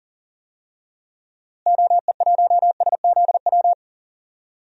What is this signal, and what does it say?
Morse code, heard as a keyed tone, about 40 wpm, 700 Hz. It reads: OE1SZW
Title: DXCC-100 Call Signs - 6 Characters - Repeated 3x - 40wpm